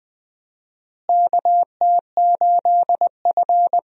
Morse code, heard as a keyed tone, about 20 wpm, 700 Hz. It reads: KT8F
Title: USA Call Signs - 4 Characters - Repeated 3x - 20wpm